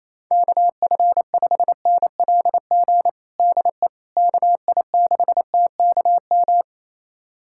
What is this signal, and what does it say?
Morse code, heard as a keyed tone, about 27 wpm, 700 Hz. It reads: KF5NLG DE KI6TKM
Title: USA Callsign DE Callsign - 6 Characters - 28wpm